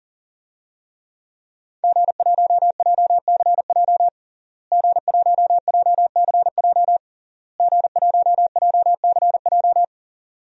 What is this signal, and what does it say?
Morse code, heard as a keyed tone, about 40 wpm, 700 Hz. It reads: G1JCJ G1JCJ G1JCJ
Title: DXCC-100 Call Signs - 5 Characters - Repeated 3x - 40wpm